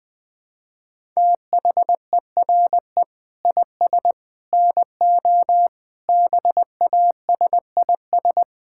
Morse code, heard as a keyed tone, about 20 wpm, 700 Hz. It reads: THERE IS NO BASIS
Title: Sentences from Top 1000 Words - Review Part 5 - 20wpm